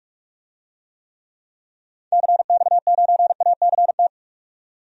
Morse code, heard as a keyed tone, about 43 wpm, 700 Hz. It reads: CX9ACT
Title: DXCC-100 Call Signs - 6 Characters - 45wpm